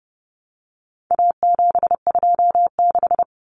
Morse code, heard as a keyed tone, about 30 wpm, 700 Hz. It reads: A726
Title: Common Cut Numbers - 4-Digit 30wpm